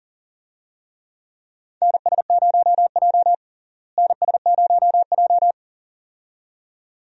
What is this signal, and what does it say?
Morse code, heard as a keyed tone, about 40 wpm, 700 Hz. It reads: NS0J NS0J
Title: USA Call Signs - 4 Characters - Repeated 2x - 40wpm